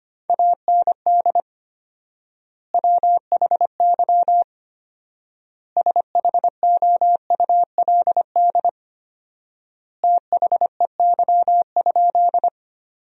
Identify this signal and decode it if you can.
Morse code, heard as a keyed tone, about 25 wpm, 700 Hz. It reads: AND WHY SHOULD THEY?
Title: 4x Word-Spacing - Sentences from Top 300 Words - 25wpm